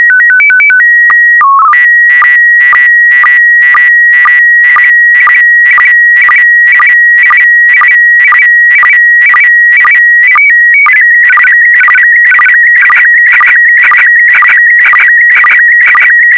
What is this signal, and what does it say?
SSTV (slow-scan television), PD 120.